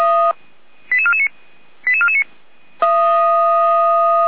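Five-tone selective call (ZVEI), heard as quick runs of five tones, tones of about 70 ms. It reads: 88449 88449
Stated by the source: ZVEI five-tone call received on SDR.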